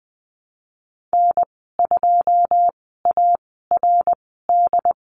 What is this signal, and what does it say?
Morse code, heard as a keyed tone, about 20 wpm, spaced as for 15 wpm, 700 Hz. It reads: N2ARD
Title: 082 Lesson 14g - Number 2 - Call Signs 20wpm